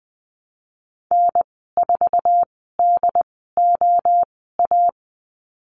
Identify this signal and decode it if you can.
Morse code, heard as a keyed tone, about 20 wpm, spaced as for 14 wpm, 700 Hz. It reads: N4DOA
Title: 069 Lesson 12g - Letter d - Call Signs 20wpm